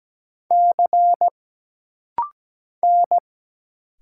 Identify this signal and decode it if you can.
Morse code, heard as a keyed tone, about 17 wpm, 700 Hz. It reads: C N
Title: Single Letters - Rapid-Fire 17wpm